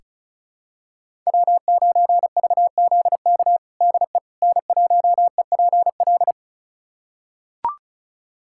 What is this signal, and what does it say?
Morse code, heard as a keyed tone, about 36 wpm, 700 Hz. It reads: W9VZK DE N1EPL
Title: USA Callsign DE Callsign - 5 Characters - 35wpm